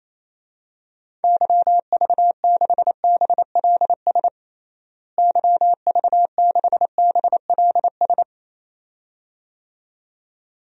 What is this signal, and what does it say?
Morse code, heard as a keyed tone, about 28 wpm, 700 Hz. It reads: YV6BLS YV6BLS
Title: DXCC-100 Call Signs - 6 Characters - Repeated 2x - 28wpm